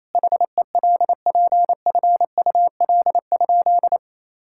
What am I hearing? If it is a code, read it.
Morse code, heard as a keyed tone, about 28 wpm, 700 Hz. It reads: HELPFUL?
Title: Sentences from Top 1000 Words - Review Part 6 - 28wpm